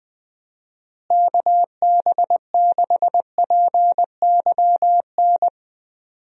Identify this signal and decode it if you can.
Morse code, heard as a keyed tone, about 20 wpm, 700 Hz. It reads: KB6PYN
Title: USA Call Signs - 6 Characters - Repeated 2x - 20wpm